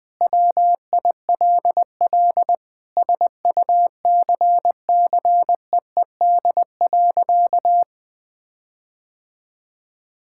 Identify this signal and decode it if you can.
Morse code, heard as a keyed tone, about 20 wpm, 700 Hz. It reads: WILL SUCCEED.
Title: Sentences with 9-Word Phrases - 20wpm